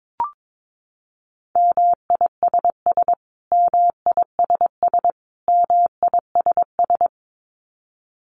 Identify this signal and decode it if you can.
Morse code, heard as a keyed tone, about 22 wpm, 700 Hz. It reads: MISS MISS MISS
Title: Top 500 Words Repeated 22wpm